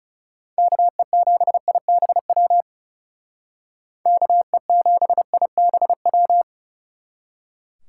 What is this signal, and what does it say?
Morse code, heard as a keyed tone, about 39 wpm, 700 Hz. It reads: KE7IBW KE7IBW